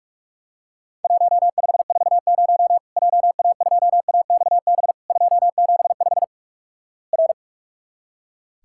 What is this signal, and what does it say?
Morse code, heard as a keyed tone, about 46 wpm, 700 Hz. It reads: 1540 JA2AXB 275 R
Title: Worked All Europe DX Contest - QTC Exchange - 45wpm